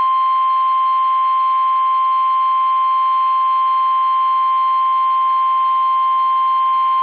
Emergency Alert System (SAME) tones (the 1050 Hz attention tone). It is heard from 0.3 to 5.8 s.